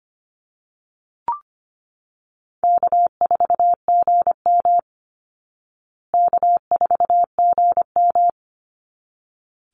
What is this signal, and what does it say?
Morse code, heard as a keyed tone, about 25 wpm, 700 Hz. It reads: K4GM K4GM
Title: Easy Call Signs Repeated 2x - 25wpm